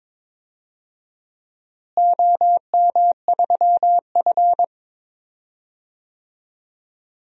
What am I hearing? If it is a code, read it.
Morse code, heard as a keyed tone, about 22 wpm, 700 Hz. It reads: OM3F
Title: DXCC-100 Call Signs - 4 Characters - Repeated 2x - 22wpm